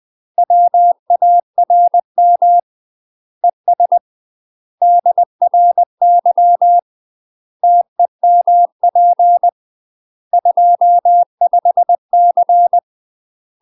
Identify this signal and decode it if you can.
Morse code, heard as a keyed tone, about 20 wpm, 700 Hz. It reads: WARM ES DRY TEMP 25C
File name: UK MASTER QSOs TRAINING / 20WPM / UK MASTER QSOs 0062